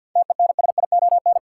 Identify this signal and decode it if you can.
Morse code, heard as a keyed tone, about 53 wpm, 700 Hz. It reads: TENSION
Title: Sentences from Top 1600 Words - New Word Drill Alt - 50wpm